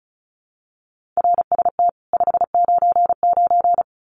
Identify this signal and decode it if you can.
Morse code, heard as a keyed tone, about 34 wpm, 700 Hz. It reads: RST 599
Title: Common QSO Phrases 35wpm